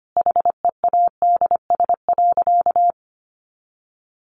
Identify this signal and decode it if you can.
Morse code, heard as a keyed tone, about 25 wpm, 700 Hz. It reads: HEADS.
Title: A Sound Of Thunder 25wpm